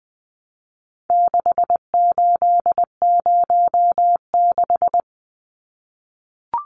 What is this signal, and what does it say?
Morse code, heard as a keyed tone, about 20 wpm, 700 Hz. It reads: 6806
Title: Numbers - 4-Digit 20wpm